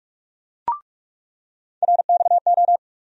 Morse code, heard as a keyed tone, about 42 wpm, 700 Hz. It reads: RXO
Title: USA Call Sign Postfixes - 3 Letters 45wpm